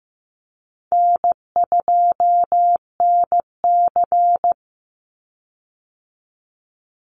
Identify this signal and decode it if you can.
Morse code, heard as a keyed tone, about 15 wpm, 700 Hz. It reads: N2NC
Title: Call Signs Repeated 3x - 15wpm